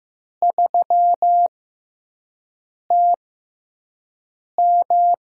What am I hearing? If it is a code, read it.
Morse code, heard as a keyed tone, about 15 wpm, spaced as for 5 wpm, 700 Hz. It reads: 3TM
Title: USA Call Signs - 4 Characters - 6x Character Spacing - 15wpm